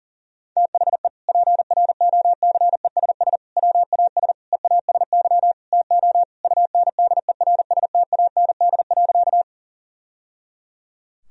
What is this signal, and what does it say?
Morse code, heard as a keyed tone, about 40 wpm, 700 Hz. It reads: THE PROCESS WAS EASY TO UNDERSTAND.
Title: Sentences from Top 1000 Words - Review Part 3 - 40wpm